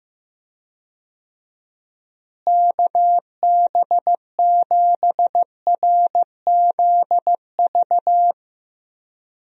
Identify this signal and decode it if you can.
Morse code, heard as a keyed tone, about 15 wpm, 700 Hz. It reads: KB7RZV
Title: USA Call Signs - 6 Characters - Repeated 2x - 15wpm